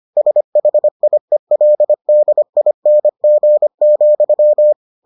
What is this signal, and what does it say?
Morse code, heard as a keyed tone, about 25 wpm, 600 Hz. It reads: SHIELDING,